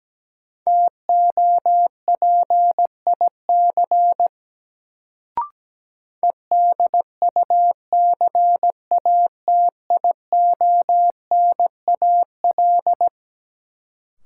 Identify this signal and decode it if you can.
Morse code, heard as a keyed tone, about 17 wpm, 700 Hz. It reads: TOPIC EDUCATIONAL
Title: Sentences from Top 1200 Words - New Word Drill - 17wpm